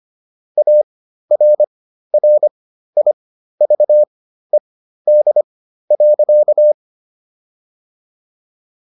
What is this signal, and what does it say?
Morse code, heard as a keyed tone, about 25 wpm, spaced as for 13 wpm, 600 Hz. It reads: ARRIVED.